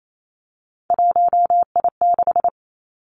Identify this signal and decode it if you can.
Morse code, heard as a keyed tone, about 28 wpm, 700 Hz. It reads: 1I6
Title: Three Challenging Letter-Number Combinations 28wpm